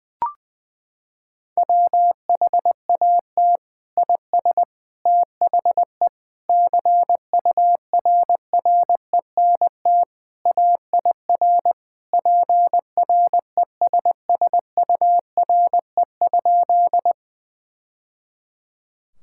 Morse code, heard as a keyed tone, about 20 wpm, 700 Hz. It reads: WHAT IS THE CURRENT AIR PRESSURE?
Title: Sentences from Top 1000 Words - Review Part 6 - 20wpm